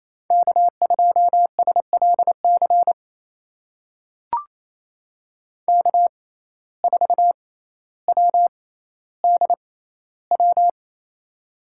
Morse code, heard as a keyed tone, about 28 wpm, spaced as for 14 wpm, 700 Hz. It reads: K2SLC K4WDW
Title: USA Call Signs - 5 Characters - 6x Character Spacing - 28wpm